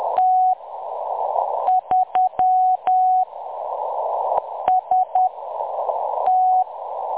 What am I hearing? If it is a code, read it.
Morse code, heard as a keyed tone, about 10 wpm, spaced as for 6 wpm, 740 Hz. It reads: T3 ST
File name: M23-ST3- -0858utc-10184khz